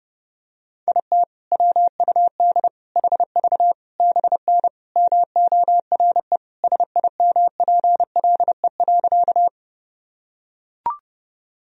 Morse code, heard as a keyed tone, about 30 wpm, 700 Hz. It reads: IT WUD HV BN MORE SIMPLE.
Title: Sentences from Top 400 Words and Abbreviations 30wpm